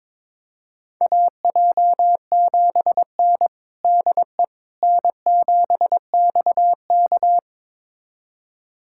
Morse code, heard as a keyed tone, about 22 wpm, 700 Hz. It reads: AJ7N DE N7XK
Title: USA Callsign DE Callsign - 4 Characters - 22wpm